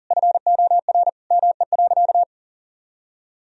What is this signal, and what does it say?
Morse code, heard as a keyed tone, about 40 wpm, 700 Hz. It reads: FOR ME.
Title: Sentences from Top 1000 Words - Review Part 1 - 40wpm